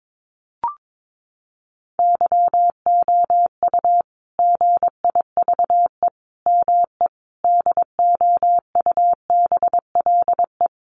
Morse code, heard as a keyed tone, about 22 wpm, 700 Hz. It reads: YOU GIVE ME DOUBLE
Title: Sentences from Top 1400 Words 22wpm